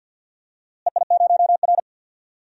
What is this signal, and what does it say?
Morse code, heard as a keyed tone, about 47 wpm, 700 Hz. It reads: EI0R
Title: DXCC-100 Call Signs - 4 Characters - Repeated 3x - 50wpm